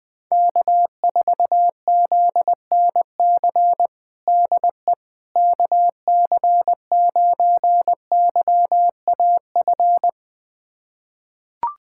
Morse code, heard as a keyed tone, about 20 wpm, 700 Hz. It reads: K4ZNC DE KC9YAF
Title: USA Callsign DE Callsign 20wpm